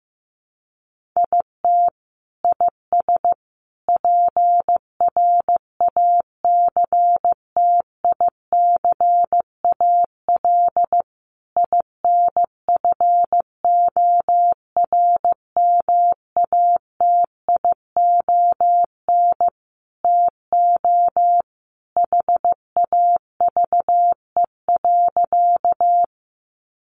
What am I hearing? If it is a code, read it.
Morse code, heard as a keyed tone, about 15 wpm, 700 Hz. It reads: IT IS PRACTICAL INFORMATION TO HAVE.
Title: Sentences from Top 1400 Words 15wpm